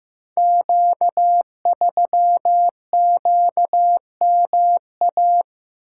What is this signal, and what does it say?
Morse code, heard as a keyed tone, about 15 wpm, 700 Hz. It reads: Q3QMA
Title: Code Groups - 5 Letter:Numbers - 15wpm